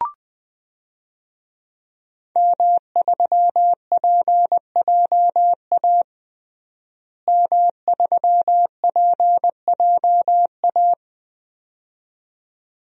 Morse code, heard as a keyed tone, about 20 wpm, 700 Hz. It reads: M3PJA M3PJA
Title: DXCC-100 Call Signs - 5 Characters - Repeated 2x - 20wpm